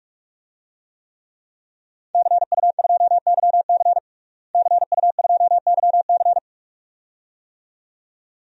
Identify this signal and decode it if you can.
Morse code, heard as a keyed tone, about 46 wpm, 700 Hz. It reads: CU2YC CU2YC
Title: DXCC-100 Call Signs - 5 Characters - Repeated 2x - 45wpm